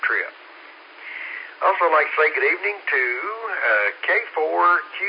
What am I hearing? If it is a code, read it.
Emergency Alert System (SAME) tones, heard as harsh buzzing bursts, with speech. It is heard from 0.9 to 4.0 s.